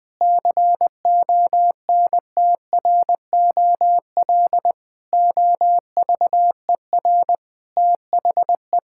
Morse code, heard as a keyed tone, about 20 wpm, 700 Hz. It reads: CONTROL OVER THE
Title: Sentences from Top 1000 Words - Review Part 5 - 20wpm